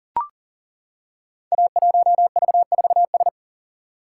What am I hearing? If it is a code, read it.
Morse code, heard as a keyed tone, about 38 wpm, 700 Hz. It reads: A1V4S
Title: Code Groups - 5 Letter:Numbers - 40wpm